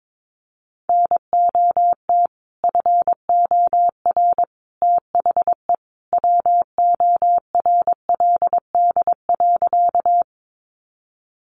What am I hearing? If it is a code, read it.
Morse code, heard as a keyed tone, about 22 wpm, 700 Hz. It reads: NOT FOR THE WORLD.
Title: Sentences from Top 600 Words 22wpm